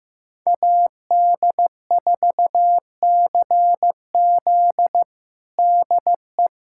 Morse code, heard as a keyed tone, about 16 wpm, 700 Hz. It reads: AD4CZ DE
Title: USA Callsign DE Callsign - 5 Characters - 15wpm